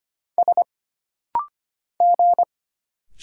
Morse code, heard as a keyed tone, about 25 wpm, 700 Hz. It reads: S G
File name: Single Character (Letter:Number:Punctuation:Prosign) - Rapid-Fire - 25wpm